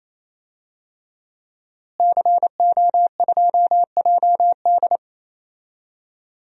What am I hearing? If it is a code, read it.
Morse code, heard as a keyed tone, about 28 wpm, 700 Hz. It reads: CO2JD
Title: DXCC-100 Call Signs - 5 Characters - Repeated 3x - 28wpm